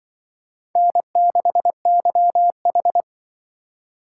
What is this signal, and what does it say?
Morse code, heard as a keyed tone, about 24 wpm, 700 Hz. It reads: N6YH
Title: Speed Racing - USA 1x2 Call Signs 20wpm